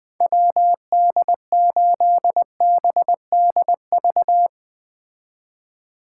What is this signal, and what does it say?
Morse code, heard as a keyed tone, about 20 wpm, 700 Hz. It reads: WD8BDV